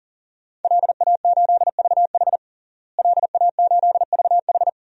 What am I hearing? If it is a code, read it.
Morse code, heard as a keyed tone, about 40 wpm, 700 Hz. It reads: LA8VH LA8VH
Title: DXCC-100 Call Signs - 5 Characters - Repeated 2x - 40wpm